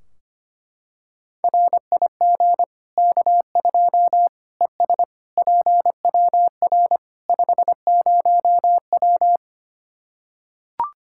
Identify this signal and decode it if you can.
Morse code, heard as a keyed tone, about 25 wpm, 700 Hz. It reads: RIG K2 ES PWR 50W
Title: Practice QSO - Exchange 2 - Info - 25wpm